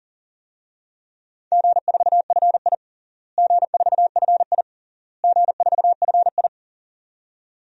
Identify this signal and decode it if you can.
Morse code, heard as a keyed tone, about 40 wpm, 700 Hz. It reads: G4FI G4FI G4FI
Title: DXCC-100 Call Signs - 4 Characters - Repeated 3x - 40wpm